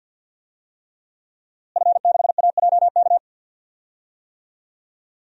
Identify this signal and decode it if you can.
Morse code, heard as a keyed tone, about 50 wpm, 700 Hz. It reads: F6AJK
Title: DXCC-100 Call Signs - 5 Characters - Repeated 3x - 50wpm